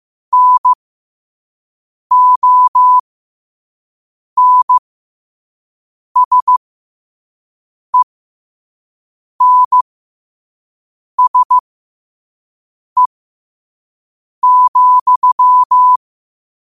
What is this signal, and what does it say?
Morse code, heard as a keyed tone, about 15 wpm, spaced as for 5 wpm, 1000 Hz. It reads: NONSENSE,
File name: qotdfiles / 5wpm